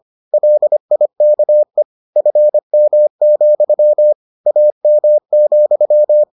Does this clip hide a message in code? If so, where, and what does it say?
Yes: Morse code from 0.3 to 6.2 s, heard as a keyed tone, about 25 wpm, 600 Hz. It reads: LIKE FM, AM,